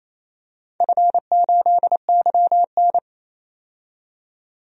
Morse code, heard as a keyed tone, about 29 wpm, 700 Hz. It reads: F8YN
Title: DXCC-100 Call Signs - 4 Characters - 28wpm